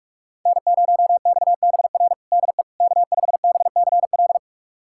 Morse code, heard as a keyed tone, about 46 wpm, 700 Hz. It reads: N0XBR DE K5BCL
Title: USA Callsign DE Callsign - 5 Characters - 45wpm